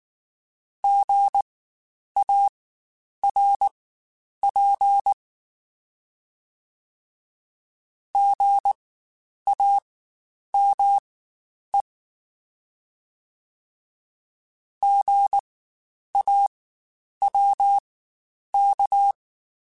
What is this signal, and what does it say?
Morse code, heard as a keyed tone, about 19 wpm, spaced as for 9 wpm, 800 Hz. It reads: GARP GAME GAWK
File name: Spaced Alphas / Spaced Alphas mp3s / Gs sp @18